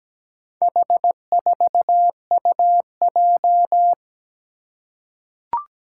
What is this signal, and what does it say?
Morse code, heard as a keyed tone, about 17 wpm, 700 Hz. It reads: H4UJ
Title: Four Challenging Letter-Number Combinations 17wpm